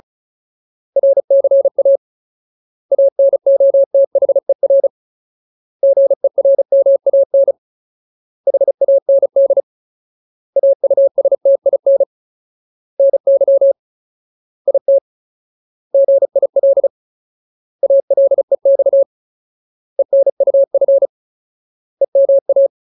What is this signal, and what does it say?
Morse code, heard as a keyed tone, about 34 wpm, 550 Hz. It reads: RCA ANOTHER GERMAN HAND AUSTIN NY IT GIL ALEX ENUF EMA
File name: Ultimate Word List / 35WPM Ultimate